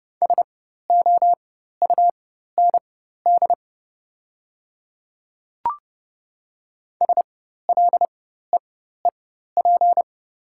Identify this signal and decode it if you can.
Morse code, heard as a keyed tone, about 30 wpm, spaced as for 14 wpm, 700 Hz. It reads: SOUND SLEEP
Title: 4x Character Spacing - Top 500 Words - 30wpm